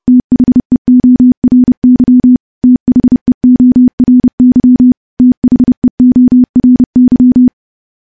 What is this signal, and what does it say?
Morse code, heard as a keyed tone, about 28 wpm, 250 Hz. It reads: THEORY THEORY THEORY
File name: morse-training-30WPM